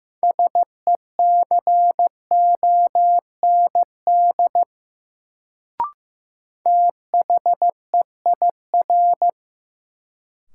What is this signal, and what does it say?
Morse code, heard as a keyed tone, about 15 wpm, 700 Hz. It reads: SECOND THEIR
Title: Sentences from Top 300 Words - New Word Drill - 15wpm